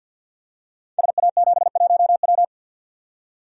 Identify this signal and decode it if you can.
Morse code, heard as a keyed tone, about 53 wpm, 700 Hz. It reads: SA71W
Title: Code Groups - 5 Letter:Numbers - 50wpm